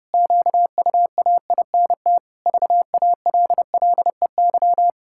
Morse code, heard as a keyed tone, about 32 wpm, 700 Hz. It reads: QUAINT VALLEY
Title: Pangram Challenge v2 - 30wpm